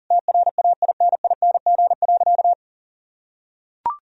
Morse code, heard as a keyed tone, about 37 wpm, 700 Hz. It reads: TRAINING.
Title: Sentences from Top 1000 Words - Review Part 4 - 40wpm